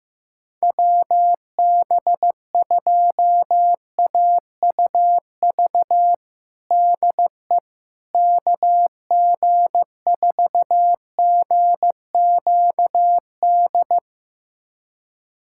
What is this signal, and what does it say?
Morse code, heard as a keyed tone, about 15 wpm, 700 Hz. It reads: WB2AUV DE KG4GQD
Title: USA Callsign DE Callsign - 6 Characters - 15wpm